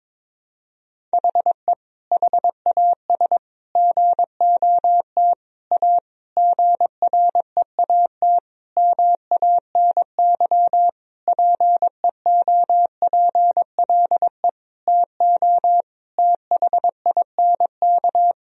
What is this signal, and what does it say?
Morse code, heard as a keyed tone, about 22 wpm, 700 Hz. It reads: HE HAS GOT A GREAT MANY PEOPLE TO THINK
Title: Sentences from Top 1000 Words - Review Part 1 - 22wpm